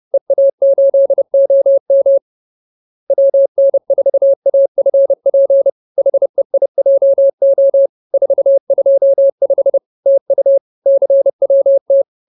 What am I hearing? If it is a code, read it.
Morse code, heard as a keyed tone, about 30 wpm, 550 Hz. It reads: EA8OM WN4AFP HEIJO 425 TU CWT
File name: CWT-213-30